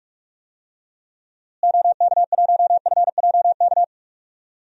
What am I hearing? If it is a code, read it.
Morse code, heard as a keyed tone, about 46 wpm, 700 Hz. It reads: OK1FJK
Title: DXCC-100 Call Signs - 6 Characters - Repeated 3x - 45wpm